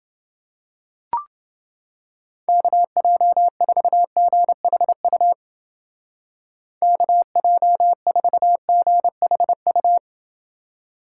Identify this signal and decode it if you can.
Morse code, heard as a keyed tone, about 28 wpm, 700 Hz. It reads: KJ4GHU KJ4GHU